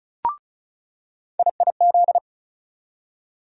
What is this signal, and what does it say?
Morse code, heard as a keyed tone, about 36 wpm, 700 Hz. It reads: IIZ
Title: USA Call Sign Postfixes - 3 Letters 35wpm